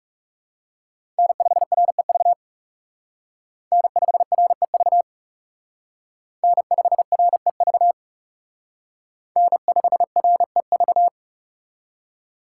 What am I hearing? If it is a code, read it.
Morse code, heard as a keyed tone, about 36 wpm, 700 Hz. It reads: N5REV N5REV N5REV N5REV
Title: Speed Racing - USA 1x3 Call Signs 30wpm